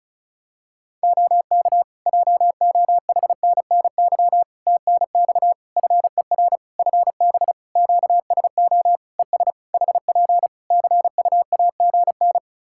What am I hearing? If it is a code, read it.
Morse code, heard as a keyed tone, about 36 wpm, 700 Hz. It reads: OK JOHNNY TNX FER FB QSO ES HP CUAGN
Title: Practice QSO - Exchange 4 - Ending - 35wpm